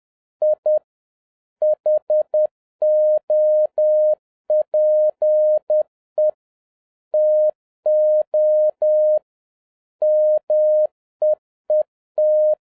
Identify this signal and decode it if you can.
Morse code, heard as a keyed tone, about 10 wpm, 610 Hz. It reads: I HOPE TO MEET
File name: 2011-07-mp3 / n21 GREETINGS FROM SINGAPORE 10wpm